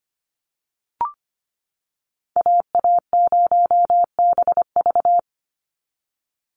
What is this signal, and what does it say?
Morse code, heard as a keyed tone, about 25 wpm, 700 Hz. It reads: AA0BV
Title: NA SOTA Activator Callsigns 25wpm